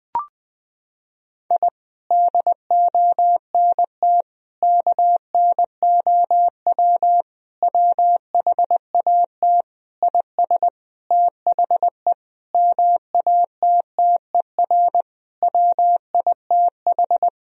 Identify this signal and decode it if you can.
Morse code, heard as a keyed tone, about 20 wpm, 700 Hz. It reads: I DONT KNOW WHAT IS THE MATTER WITH
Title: Sentences from Top 1000 Words - Review Part 4 - 20wpm